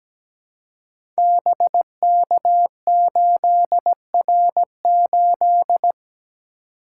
Morse code, heard as a keyed tone, about 17 wpm, 700 Hz. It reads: BK8R8
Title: Code Groups - 5 Letter:Numbers - 17wpm